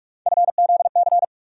Morse code, heard as a keyed tone, about 46 wpm, 700 Hz. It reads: FZC
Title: USA Call Sign Postfixes - 3 Letters 45wpm